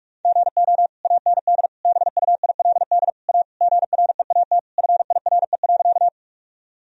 Morse code, heard as a keyed tone, about 42 wpm, 700 Hz. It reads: GO AND BUILD A GREAT FIRE.
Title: Sentences from Top 1000 Words - Review Part 1 - 45wpm